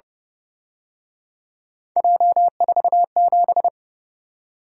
Morse code, heard as a keyed tone, about 29 wpm, 700 Hz. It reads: J47